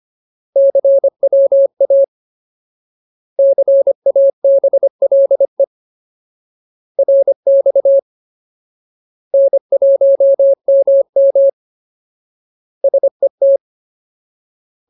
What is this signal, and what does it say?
Morse code, heard as a keyed tone, about 25 wpm, 550 Hz. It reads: CWA CABLE RX N1MM SET
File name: Ultimate Word List / 25WPM Ultimate